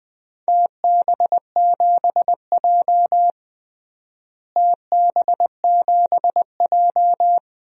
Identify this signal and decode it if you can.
Morse code, heard as a keyed tone, about 20 wpm, 700 Hz. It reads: TB7J TB7J
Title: DXCC-100 Call Signs - 4 Characters - Repeated 2x - 20wpm